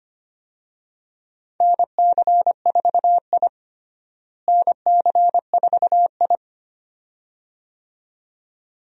Morse code, heard as a keyed tone, about 25 wpm, 700 Hz. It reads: NC4I NC4I
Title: USA Call Signs - 4 Characters - Repeated 2x - 25wpm